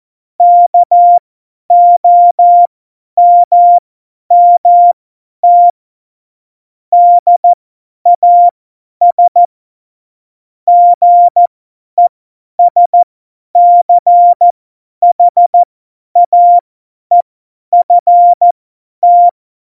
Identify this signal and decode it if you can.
Morse code, heard as a keyed tone, about 14 wpm, spaced as for 10 wpm, 700 Hz. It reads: KOMMT DAS GESCHAEFT
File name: Morsekurs / Witze / Witz 05. Februar 2019, 50 F70